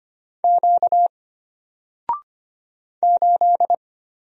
Morse code, heard as a keyed tone, about 25 wpm, 700 Hz. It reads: Q 8
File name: Single-Letter-Number-25wpm